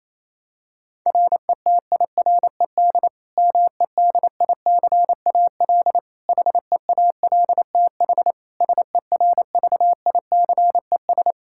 Morse code, heard as a keyed tone, about 28 wpm, 700 Hz. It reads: RETIRED MEDICAL HEALTH SERVICES
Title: Practice QSO - Exchange 3 - More Info - 28wpm